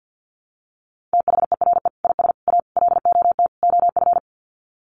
Morse code, heard as a keyed tone, about 48 wpm, 700 Hz. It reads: THERE IS A LOT OF
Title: Sets of 5 Words 50wpm